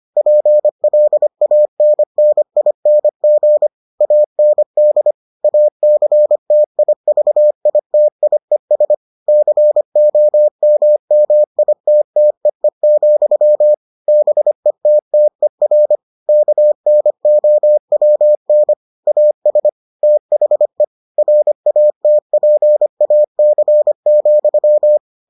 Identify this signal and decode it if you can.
Morse code, heard as a keyed tone, about 25 wpm, 600 Hz. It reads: PLANNING AND ACTIVITIES COMMITTEE, BETTER KNOWN AS THE RATPAC,